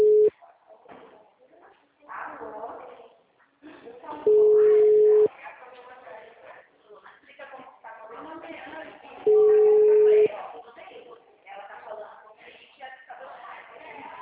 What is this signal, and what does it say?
Ringback tone (Europe, 425 Hz), with speech.